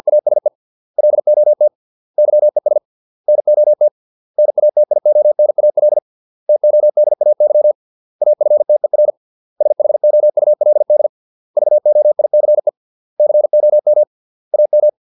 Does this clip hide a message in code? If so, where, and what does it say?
Yes: Morse code from 0.1 to 14.9 s, heard as a keyed tone, about 50 wpm, 600 Hz. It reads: ASE LOT YES NOT NATIONAL TODAY AFTER SHOULD VOICE COM AM